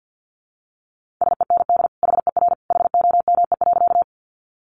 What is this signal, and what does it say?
Morse code, heard as a keyed tone, about 50 wpm, 700 Hz. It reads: SEND HER HOME.
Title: Sentences from Top 1500 Words 50wpm